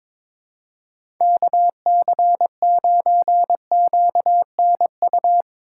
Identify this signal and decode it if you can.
Morse code, heard as a keyed tone, about 22 wpm, 700 Hz. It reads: KC9QNU
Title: USA Call Signs - 6 Characters - 2x Character Spacing - 22wpm